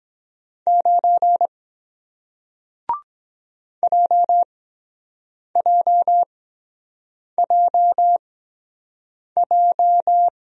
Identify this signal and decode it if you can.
Morse code, heard as a keyed tone, about 23 wpm, spaced as for 7 wpm, 700 Hz. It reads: 9 JJJJ